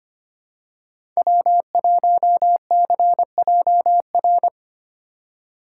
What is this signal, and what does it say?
Morse code, heard as a keyed tone, about 25 wpm, 700 Hz. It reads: W1CJR
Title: USA Call Signs - 5 Characters - 4x Character Spacing - 25wpm